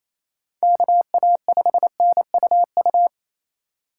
Morse code, heard as a keyed tone, about 36 wpm, 700 Hz. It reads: KA5NUU